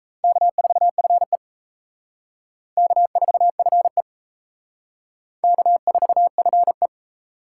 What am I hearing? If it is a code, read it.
Morse code, heard as a keyed tone, about 35 wpm, 700 Hz. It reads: K4FE K4FE K4FE
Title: Speed Racing - USA Call Signs 28wpm